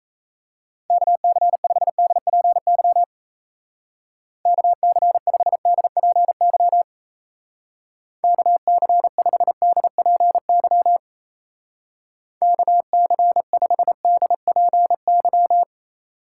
Morse code, heard as a keyed tone, about 35 wpm, 700 Hz. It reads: KC5DPY KC5DPY KC5DPY KC5DPY
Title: Speed Racing - USA 2x3 Call Signs 28wpm